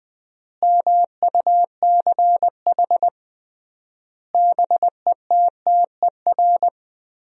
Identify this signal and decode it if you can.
Morse code, heard as a keyed tone, about 21 wpm, 700 Hz. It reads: MUCH BETTER
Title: 3x Word-Spacing - Sentences from Top 400 Words - 20wpm